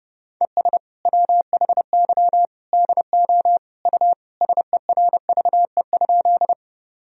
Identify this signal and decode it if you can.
Morse code, heard as a keyed tone, about 30 wpm, 700 Hz. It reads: ES WHY DO U SERVE?
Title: Sentences from Top 400 Words and Abbreviations 30wpm